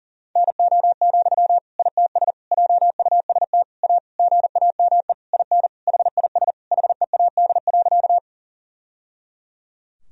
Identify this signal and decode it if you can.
Morse code, heard as a keyed tone, about 40 wpm, 700 Hz. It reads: NO, ITS JUST A GAME IN HIS HEAD.
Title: Sentences from Top 1000 Words - Review Part 3 - 40wpm